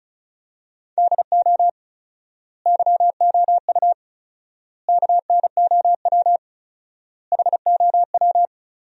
Morse code, heard as a keyed tone, about 36 wpm, 700 Hz. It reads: DO YOU KNOW HOW
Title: 4x Word-Spacing - Sentences from Top 300 Words - 35wpm